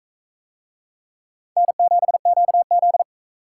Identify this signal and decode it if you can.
Morse code, heard as a keyed tone, about 40 wpm, 700 Hz. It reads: N7QZ